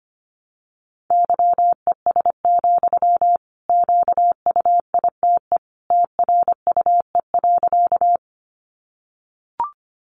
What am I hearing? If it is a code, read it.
Morse code, heard as a keyed tone, about 25 wpm, 700 Hz. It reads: YES, QUITE TRUE.
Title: Sentences from Top 700 Words 25wpm